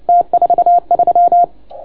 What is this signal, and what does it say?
Morse code, heard as a keyed tone, about 30 wpm, 690 Hz. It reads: T43